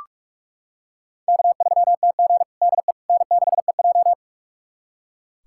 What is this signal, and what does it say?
Morse code, heard as a keyed tone, about 46 wpm, 700 Hz. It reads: K3TG DE N6EJ
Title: USA Callsign DE Callsign - 4 Characters - 45wpm